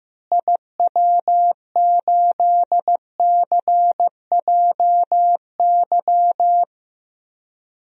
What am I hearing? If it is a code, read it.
Morse code, heard as a keyed tone, about 15 wpm, 700 Hz. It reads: IW8CJY
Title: DXCC-100 Call Signs - 6 Characters - Repeated 2x - 15wpm